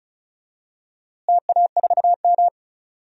Morse code, heard as a keyed tone, about 36 wpm, 700 Hz. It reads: TA4M
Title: DXCC-100 Call Signs - 4 Characters - Repeated 3x - 35wpm